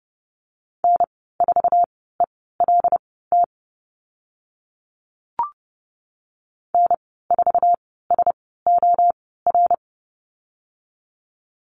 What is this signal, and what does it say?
Morse code, heard as a keyed tone, about 30 wpm, spaced as for 17 wpm, 700 Hz. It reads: N4ELT N4SOR
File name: Lesson 15g - Number 5 - Call Signs 30wpm